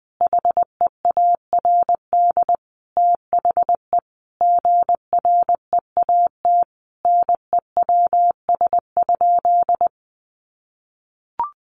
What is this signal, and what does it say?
Morse code, heard as a keyed tone, about 19 wpm, 700 Hz. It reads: HEARD THE GREAT NEWS?
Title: Sentences from Top 800 Words 20wpm